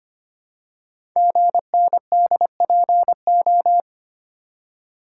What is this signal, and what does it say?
Morse code, heard as a keyed tone, about 25 wpm, 700 Hz. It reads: GNDPO
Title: Code Groups - 5 Letters - 25wpm